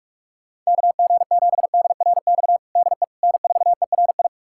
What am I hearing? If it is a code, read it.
Morse code, heard as a keyed tone, about 47 wpm, 700 Hz. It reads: KG7DRX DE N4ERI
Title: USA Callsign DE Callsign 45wpm